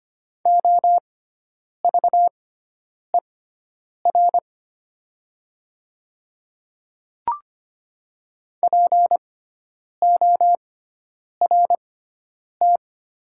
Morse code, heard as a keyed tone, about 25 wpm, spaced as for 9 wpm, 700 Hz. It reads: OVER PORT
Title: 6x Character Spacing - Top 300 Words - 25wpm